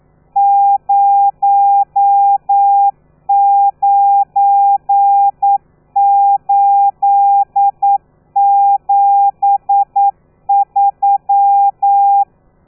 Morse code, heard as a keyed tone, about 9 wpm, 800 Hz. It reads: 09873
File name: telegrafia / i2tfj / Lez12 40